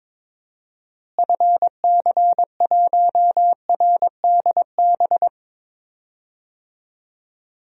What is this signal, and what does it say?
Morse code, heard as a keyed tone, about 22 wpm, 700 Hz. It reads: FC1RDB
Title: DXCC-100 Call Signs - 6 Characters - Repeated 3x - 22wpm